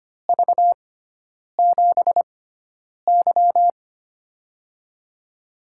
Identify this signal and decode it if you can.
Morse code, heard as a keyed tone, about 25 wpm, spaced as for 9 wpm, 700 Hz. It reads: V7Y
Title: USA Call Signs - 4 Characters - 6x Character Spacing - 25wpm